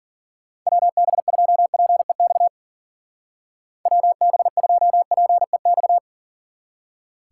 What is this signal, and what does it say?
Morse code, heard as a keyed tone, about 48 wpm, 700 Hz. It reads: WB2PEX WB2PEX
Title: Speed Racing - USA 2x3 Call Signs 40wpm